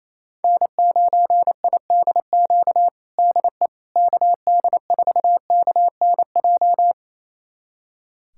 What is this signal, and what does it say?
Morse code, heard as a keyed tone, about 28 wpm, 700 Hz. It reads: N9IDQ DE KD4KNJ
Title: USA Callsign DE Callsign 28wpm